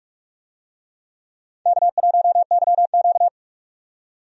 Morse code, heard as a keyed tone, about 46 wpm, 700 Hz. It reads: K1YQ
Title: USA Call Signs - 4 Characters - 45wpm